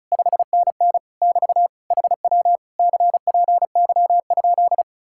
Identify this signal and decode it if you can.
Morse code, heard as a keyed tone, about 44 wpm, 700 Hz. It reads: 5NN <BT> HW CPY?